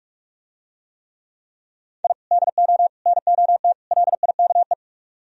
Morse code, heard as a keyed tone, about 47 wpm, 700 Hz. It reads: I DO NOT LIKE